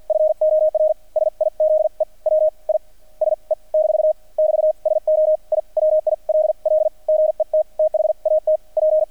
Morse code, heard as a keyed tone, about 50 wpm, spaced as for 38 wpm, 620 Hz. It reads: FOR SIZE WISE - <BT> SO I WILL GET THAT P
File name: archived / CRSnet